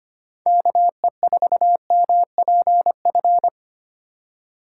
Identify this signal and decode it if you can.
Morse code, heard as a keyed tone, about 25 wpm, 700 Hz. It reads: KE4MPF
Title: USA Call Signs - 6 Characters - 6x Character Spacing - 25wpm